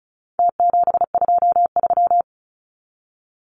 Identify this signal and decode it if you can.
Morse code, heard as a keyed tone, about 34 wpm, 700 Hz. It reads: T723
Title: Numbers - Contest Style - 4-Digit 35wpm